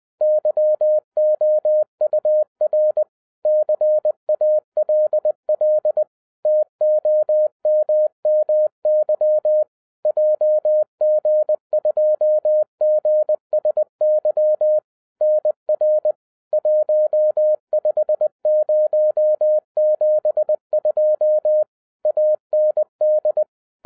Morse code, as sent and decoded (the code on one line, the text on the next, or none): -.-- --- ..- .-. / -.-. .- .-.. .-.. / - --- -- -- -.-- / .--- --. ..--- --. ... -.-- / -. .-. / .---- ..... ----- --... ..--- / .- -. -..
YOUR CALL TOMMY JG2GSY NR 15072 AND